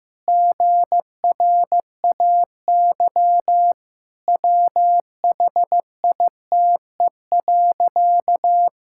Morse code, heard as a keyed tone, about 15 wpm, 700 Hz. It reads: GRAY WHITE.
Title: Pangram Challenge v2 - 15wpm